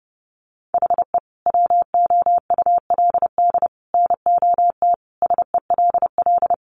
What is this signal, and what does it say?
Morse code, heard as a keyed tone, about 30 wpm, 700 Hz. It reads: HE WOULD NOT SELL
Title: Sentences from Top 1100 Words 30wpm